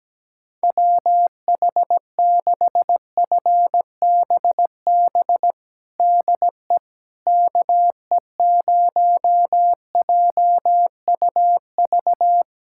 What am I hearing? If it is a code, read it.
Morse code, heard as a keyed tone, about 17 wpm, 700 Hz. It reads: WH6FBB DE KE0JUV
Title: USA Callsign DE Callsign 17wpm